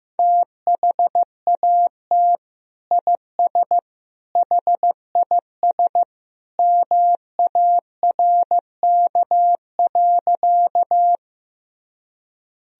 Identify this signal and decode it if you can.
Morse code, heard as a keyed tone, about 15 wpm, 700 Hz. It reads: THAT IS HIS MARK.
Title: Sentences from Top 1000 Words - Review Part 1 - 15wpm